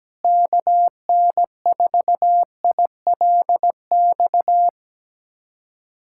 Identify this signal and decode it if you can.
Morse code, heard as a keyed tone, about 17 wpm, 700 Hz. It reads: KN4ILX
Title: USA Callsign DE Callsign - 6 Characters - 17wpm